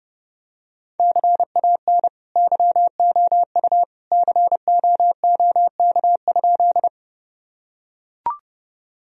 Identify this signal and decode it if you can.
Morse code, heard as a keyed tone, about 32 wpm, 700 Hz. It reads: CAN YOU COOK?
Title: Sentences from Top 1000 Words - Review Part 4 - 30wpm